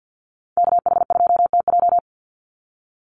Morse code, heard as a keyed tone, about 50 wpm, 700 Hz. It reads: KH2TJ
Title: NA SOTA Chaser Callsigns 50wpm